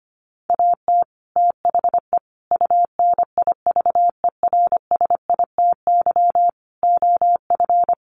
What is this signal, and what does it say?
Morse code, heard as a keyed tone, about 25 wpm, 700 Hz. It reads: AT THE UNIVERSITY OF
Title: Sets of 5 Words Encore2 25wpm